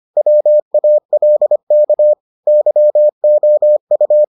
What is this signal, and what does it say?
Morse code, heard as a keyed tone, about 25 wpm, 600 Hz. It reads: WALK YOU